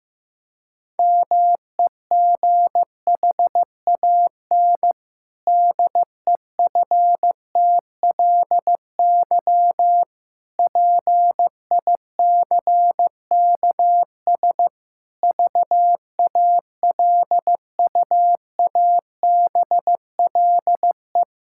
Morse code, heard as a keyed tone, about 15 wpm, 700 Hz. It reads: MEGHAN DEFTLY PICKS VALUABLE
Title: Pangram Challenge v2 - 15wpm